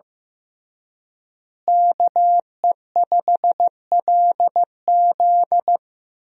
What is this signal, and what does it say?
Morse code, heard as a keyed tone, about 15 wpm, 700 Hz. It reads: KE5LZ
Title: USA Call Signs - 5 Characters - Repeated 3x - 15wpm